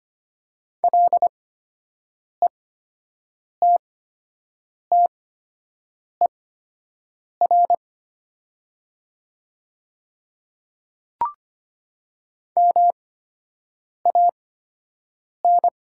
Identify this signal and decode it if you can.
Morse code, heard as a keyed tone, about 25 wpm, spaced as for 7 wpm, 700 Hz. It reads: LETTER MAN
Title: 8x Character Spacing - Top 500 Words - 25wpm